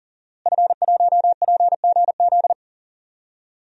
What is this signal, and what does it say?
Morse code, heard as a keyed tone, about 40 wpm, 700 Hz. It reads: F1PGZ
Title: DXCC-100 Call Signs - 5 Characters - Repeated 2x - 40wpm